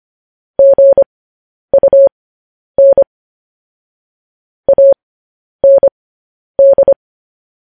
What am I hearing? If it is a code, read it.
Morse code, heard as a keyed tone, about 25 wpm, spaced as for 10 wpm, 550 Hz. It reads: GUN AND
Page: news headlines in morse code at 10 wpm